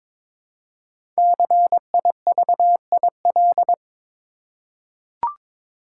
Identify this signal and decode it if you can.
Morse code, heard as a keyed tone, about 22 wpm, 700 Hz. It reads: CIVIL
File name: 5 Letter Words - First 3 Letters Given - 22wpm